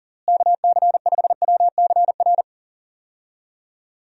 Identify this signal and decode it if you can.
Morse code, heard as a keyed tone, about 40 wpm, 700 Hz. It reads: KC5WCR
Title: USA Call Signs - 6 Characters - Repeated 2x - 40wpm